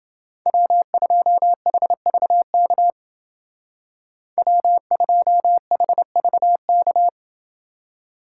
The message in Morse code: W2HVK W2HVK

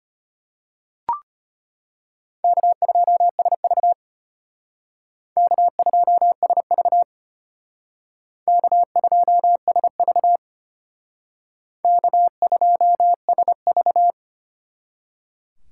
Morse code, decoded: K2SV K2SV K2SV K2SV

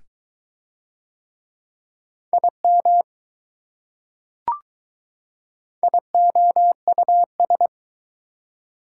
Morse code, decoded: IM IOUS

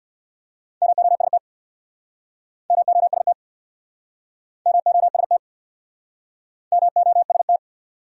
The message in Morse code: MOST MOST MOST MOST

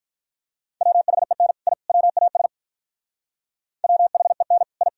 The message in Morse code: WHEN I WAS WHEN I

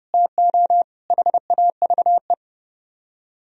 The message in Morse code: TO HAVE

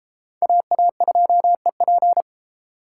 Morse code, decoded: AA2EP

